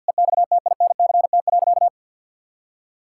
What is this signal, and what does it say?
Morse code, heard as a keyed tone, about 50 wpm, 700 Hz. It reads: EXTINCT.
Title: Pangram Challenge v2 - 50wpm